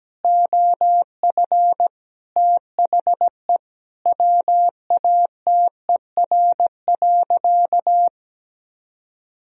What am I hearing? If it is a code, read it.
Morse code, heard as a keyed tone, about 17 wpm, 700 Hz. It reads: OF THE WATER.
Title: Sentences from Top 1000 Words - Review Part 3 - 17wpm